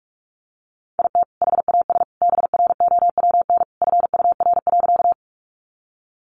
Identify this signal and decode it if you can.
Morse code, heard as a keyed tone, about 44 wpm, 700 Hz. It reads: IT HAS BROWN FUR.
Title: Easy Phrases 45wpm